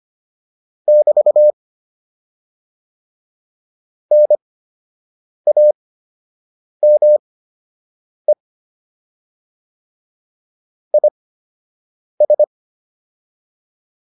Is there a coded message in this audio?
Yes: Morse code from 0.9 to 12.4 s, heard as a keyed tone, about 25 wpm, spaced as for 7 wpm, 600 Hz. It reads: <BT> NAME IS